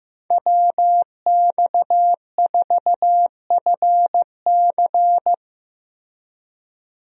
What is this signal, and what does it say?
Morse code, heard as a keyed tone, about 15 wpm, 700 Hz. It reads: WX4FC
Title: USA Call Signs - 5 Characters - Repeated 3x - 15wpm